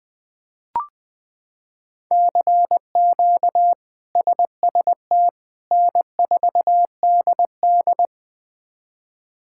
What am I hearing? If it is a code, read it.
Morse code, heard as a keyed tone, about 20 wpm, 700 Hz. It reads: CQ SST N4DD
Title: K1USN SST QSO Practice - 20wpm